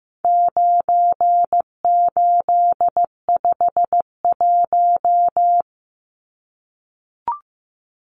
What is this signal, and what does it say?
Morse code, heard as a keyed tone, about 15 wpm, 700 Hz. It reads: 9851
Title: Numbers - 4-Digit 15wpm